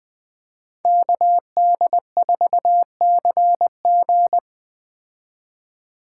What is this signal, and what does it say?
Morse code, heard as a keyed tone, about 20 wpm, 700 Hz. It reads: KD4CG
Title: Speed Racing - USA 2x2 Call Signs 17wpm